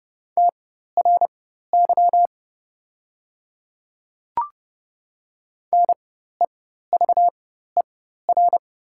Morse code, heard as a keyed tone, about 30 wpm, spaced as for 14 wpm, 700 Hz. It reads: TRY NEVER
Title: 4x Character Spacing - Top 500 Words - 30wpm